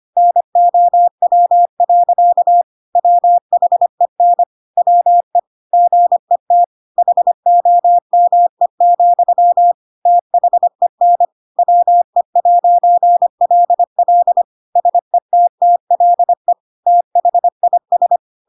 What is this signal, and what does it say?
Morse code, heard as a keyed tone, about 25 wpm, 700 Hz. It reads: NOW. WHEN WE GET HOME, THEN WE'LL SETTLE THIS
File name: Pinocchio / Pinocchio ch3 25 25 ik1hge